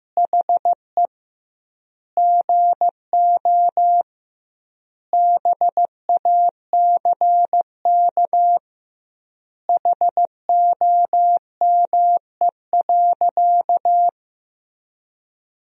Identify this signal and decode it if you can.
Morse code, heard as a keyed tone, about 15 wpm, 700 Hz. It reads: HE GO BACK HOME.
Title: 2x Word-Spacing - Sentences from Top 500 Words - 15wpm